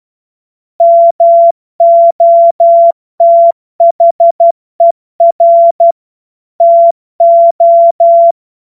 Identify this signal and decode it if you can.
Morse code, heard as a keyed tone, about 12 wpm, 680 Hz. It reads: MOTHER TO
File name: Morsekurs / Witze